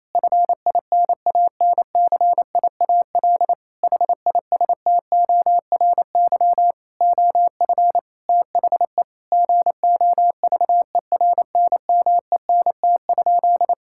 Morse code, heard as a keyed tone, about 28 wpm, 700 Hz. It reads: FINANCIAL HISTORY OF THE GOVERNMENT?
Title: Sentences from Top 1000 Words - Review Part 5 - 28wpm